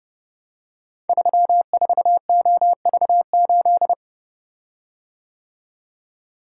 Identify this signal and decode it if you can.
Morse code, heard as a keyed tone, about 30 wpm, 700 Hz. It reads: 34OV8
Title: Code Groups - 5 Letter:Numbers - 30wpm